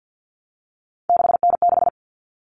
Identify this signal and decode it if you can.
Morse code, heard as a keyed tone, about 50 wpm, 700 Hz. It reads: 6N6
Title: All Cut Numbers - 3-Digit 50wpm